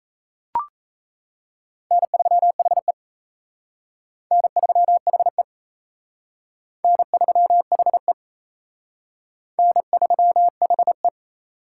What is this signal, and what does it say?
Morse code, heard as a keyed tone, about 36 wpm, 700 Hz. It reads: N3HE N3HE N3HE N3HE